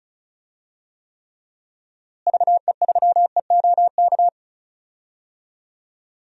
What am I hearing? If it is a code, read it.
Morse code, heard as a keyed tone, about 36 wpm, 700 Hz. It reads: VE3EOK
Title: DXCC-100 Call Signs - 6 Characters - Repeated 3x - 35wpm